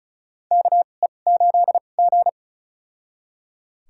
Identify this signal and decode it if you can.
Morse code, heard as a keyed tone, about 34 wpm, spaced as for 25 wpm, 700 Hz. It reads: KE8G
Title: USA Call Signs - 4 Characters - 2x Character Spacing - 35wpm